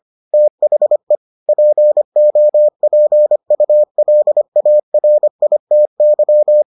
Morse code, heard as a keyed tone, about 25 wpm, 600 Hz. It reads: THE POPULARITY